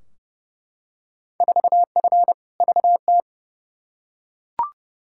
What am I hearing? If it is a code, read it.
Morse code, heard as a keyed tone, about 32 wpm, 700 Hz. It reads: 4F VT
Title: Field Day Exchanges Encore - 30wpm